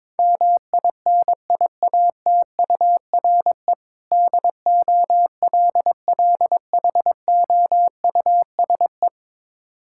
Morse code, heard as a keyed tone, about 22 wpm, 700 Hz. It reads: MINIATURE DOLLHOUSE
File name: Difficult Words with Hints - 22wpm